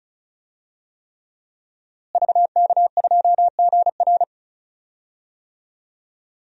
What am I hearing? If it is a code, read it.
Morse code, heard as a keyed tone, about 36 wpm, 700 Hz. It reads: VK2GR
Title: DXCC-100 Call Signs - 5 Characters - Repeated 3x - 35wpm